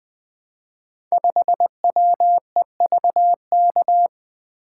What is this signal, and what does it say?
Morse code, heard as a keyed tone, about 20 wpm, 700 Hz. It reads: 5WEVK
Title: Code Groups - 5 Letter:Numbers - 20wpm